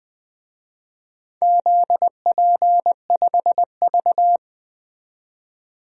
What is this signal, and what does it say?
Morse code, heard as a keyed tone, about 20 wpm, 700 Hz. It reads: ZP5V
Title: DXCC-100 Call Signs - 4 Characters - 20wpm